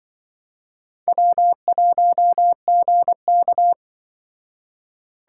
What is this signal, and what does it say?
Morse code, heard as a keyed tone, about 24 wpm, 700 Hz. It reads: W1GK
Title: Speed Racing - USA 1x2 Call Signs 20wpm